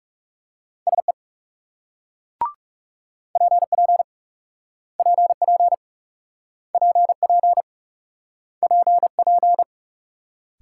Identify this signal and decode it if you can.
Morse code, heard as a keyed tone, about 40 wpm, 700 Hz. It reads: SE PP PP PP PP